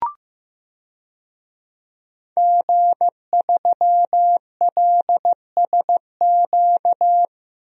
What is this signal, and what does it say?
Morse code, heard as a keyed tone, about 15 wpm, 700 Hz. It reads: G3LSQ
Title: DXCC-100 Call Signs - 5 Characters - 15wpm